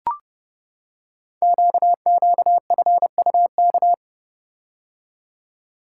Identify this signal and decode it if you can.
Morse code, heard as a keyed tone, about 30 wpm, 700 Hz. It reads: QQFUK
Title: Code Groups - 5 Letters - 30wpm